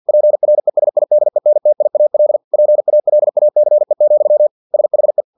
Morse code, heard as a keyed tone, about 50 wpm, 600 Hz. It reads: PRESIDENTIAL PALACE, SHE
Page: Never 49wpm